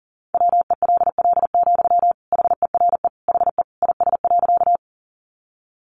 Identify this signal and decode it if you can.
Morse code, heard as a keyed tone, about 40 wpm, 700 Hz. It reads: WELL, HERE HE IS.
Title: Sentences from Top 200 Words 40wpm